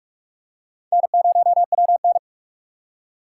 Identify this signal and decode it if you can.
Morse code, heard as a keyed tone, about 46 wpm, 700 Hz. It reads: N0WN
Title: USA Call Signs - 4 Characters - 3x Character Spacing - 45wpm